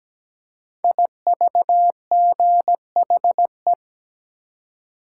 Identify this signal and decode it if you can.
Morse code, heard as a keyed tone, about 17 wpm, 700 Hz. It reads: IVGHE
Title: Code Groups - 5 Letters - 17wpm